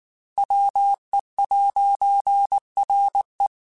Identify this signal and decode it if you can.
Morse code, heard as a keyed tone, about 19 wpm, 800 Hz. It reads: WE'RE
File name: Technical Subjects / Technical Subjects-mp3s / AND Gate @18